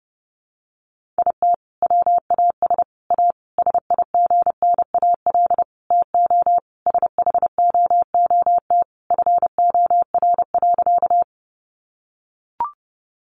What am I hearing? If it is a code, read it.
Morse code, heard as a keyed tone, about 30 wpm, 700 Hz. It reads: IT WAS A SIGNAL TO SHOOT FOR.
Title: Sentences from Top 1400 Words 30wpm